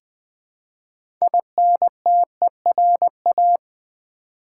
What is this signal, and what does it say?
Morse code, heard as a keyed tone, about 21 wpm, 700 Hz. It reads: INTERA
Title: Build A Word - 2 Letters At A Time - 8-Letter Words - 20wpm